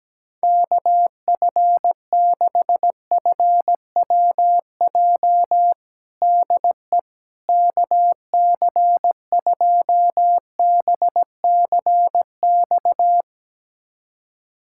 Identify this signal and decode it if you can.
Morse code, heard as a keyed tone, about 17 wpm, 700 Hz. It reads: KF6FWJ DE KC2BCX
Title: USA Callsign DE Callsign - 6 Characters - 17wpm